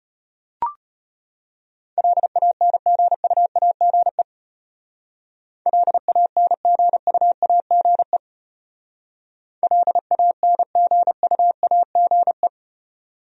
Morse code, heard as a keyed tone, about 30 wpm, 700 Hz. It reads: LANGUAGE LANGUAGE LANGUAGE
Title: Speed-Racing Top 500 Words 25wpm